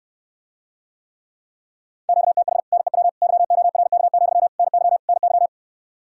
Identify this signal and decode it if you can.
Morse code, heard as a keyed tone, about 76 wpm, 700 Hz. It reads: QTH NEW YORK, NY NY